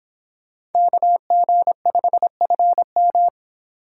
Morse code, heard as a keyed tone, about 25 wpm, 700 Hz. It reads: KG5FM